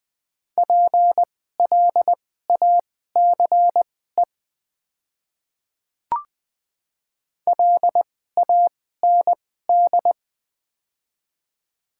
Morse code, heard as a keyed tone, about 20 wpm, spaced as for 14 wpm, 700 Hz. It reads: PLACE LAND
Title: 2x Character Spacing - Top 200 Words - 20wpm